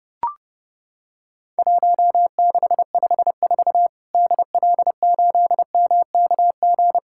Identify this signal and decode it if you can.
Morse code, heard as a keyed tone, about 32 wpm, 700 Hz. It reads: 1654 DL8MKG
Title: Worked All Europe DX Contest - QTC Exchange - 30wpm